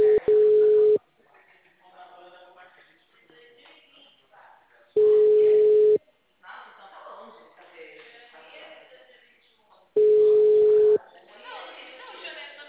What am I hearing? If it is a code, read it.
Ringback tone (Europe, 425 Hz), with speech.